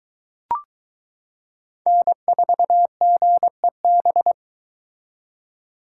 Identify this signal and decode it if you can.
Morse code, heard as a keyed tone, about 23 wpm, 700 Hz. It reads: N4GEB